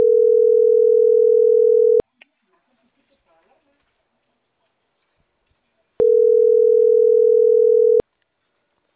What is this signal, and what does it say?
Ringback tone (North America), 440 + 480 Hz, with speech.
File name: CENTRAL CAEA / out agcaea03